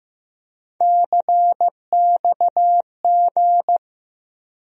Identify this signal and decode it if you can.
Morse code, heard as a keyed tone, about 16 wpm, 700 Hz. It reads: CXG
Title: USA Call Sign Postfixes - 3 Letters 15wpm